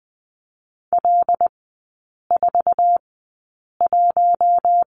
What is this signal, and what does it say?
Morse code, heard as a keyed tone, about 20 wpm, spaced as for 8 wpm, 700 Hz. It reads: L41
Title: 074 Lesson 13e - Letter L - Three Characters 20wpm